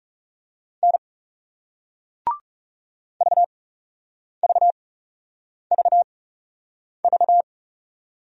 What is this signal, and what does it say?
Morse code, heard as a keyed tone, about 40 wpm, spaced as for 8 wpm, 700 Hz. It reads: N VVVV